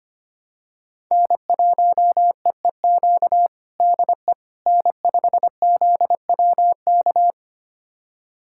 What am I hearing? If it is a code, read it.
Morse code, heard as a keyed tone, about 25 wpm, 700 Hz. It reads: N1EEQ DE N5ZWK
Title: USA Callsign DE Callsign - 5 Characters - 25wpm